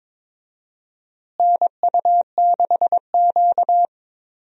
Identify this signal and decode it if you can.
Morse code, heard as a keyed tone, about 22 wpm, 700 Hz. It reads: NU6Q
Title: USA Call Signs - 4 Characters - Repeated 3x - 22wpm